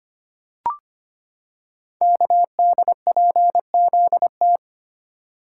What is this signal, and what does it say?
Morse code, heard as a keyed tone, about 25 wpm, 700 Hz. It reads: KDPZT
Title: Code Groups - 5 Letters - 25wpm